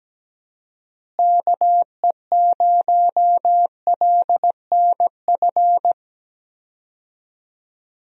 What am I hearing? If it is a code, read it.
Morse code, heard as a keyed tone, about 17 wpm, 700 Hz. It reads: KE0LNF
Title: USA Call Signs - 6 Characters - Repeated 3x - 17wpm